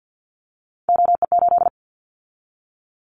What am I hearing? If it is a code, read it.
Morse code, heard as a keyed tone, about 50 wpm, 700 Hz. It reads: OE8
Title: Callsign Prefixes for Top 100 DXCC - 50wpm